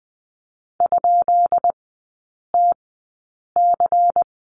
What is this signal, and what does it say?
Morse code, heard as a keyed tone, about 20 wpm, spaced as for 8 wpm, 700 Hz. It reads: ?TC
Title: 133 Lesson 22e - Question Mark - Three Characters 20wpm